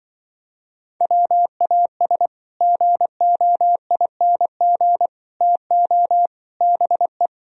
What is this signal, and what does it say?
Morse code, heard as a keyed tone, about 24 wpm, 700 Hz. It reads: WAS GOING TO BE
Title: Speed-Racing - Sets of 4 Words 20wpm